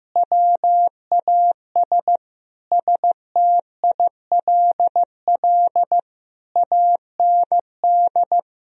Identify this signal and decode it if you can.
Morse code, heard as a keyed tone, about 15 wpm, 700 Hz. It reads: WAS STILL AND
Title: Sentences from Top 1000 Words - Review Part 2 - 15wpm